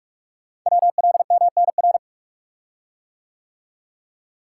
Morse code, heard as a keyed tone, about 44 wpm, 690 Hz. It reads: WLMNR
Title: Code Groups - 5 Letters - 45wpm